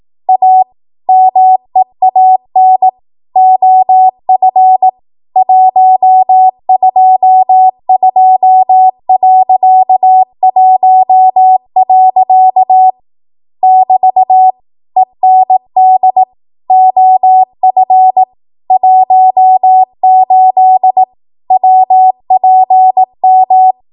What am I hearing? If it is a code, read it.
Morse code, heard as a keyed tone, about 18 wpm, 750 Hz. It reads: A MEAN OF 122.1. <BT> END OF 18 WPM
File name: Morse / 18WPM